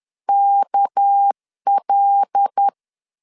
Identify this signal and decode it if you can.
Morse code, heard as a keyed tone, about 10 wpm, 800 Hz. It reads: KL
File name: LEZIONI / 6 PKX0 / 6 PK 10wpm 2436